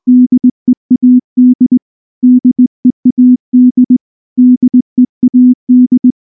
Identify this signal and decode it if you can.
Morse code, heard as a keyed tone, about 19 wpm, 250 Hz. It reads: DEAD DEAD DEAD
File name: morse / morse-training-20WPM-1